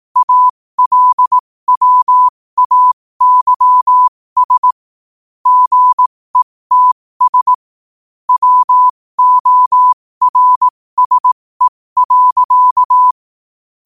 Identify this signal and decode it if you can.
Morse code, heard as a keyed tone, about 18 wpm, spaced as for 15 wpm, 1000 Hz. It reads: ALWAYS GETS WORSE.
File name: qotdfiles / 15wpm